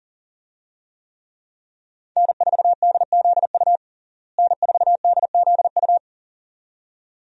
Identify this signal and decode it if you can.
Morse code, heard as a keyed tone, about 40 wpm, 700 Hz. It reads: N4DZU N4DZU
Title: USA Call Signs - 5 Characters - Repeated 2x - 40wpm